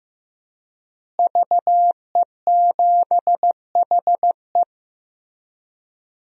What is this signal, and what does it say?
Morse code, heard as a keyed tone, about 15 wpm, 700 Hz. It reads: VE7HE
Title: DXCC-100 Call Signs - 5 Characters - Repeated 2x - 15wpm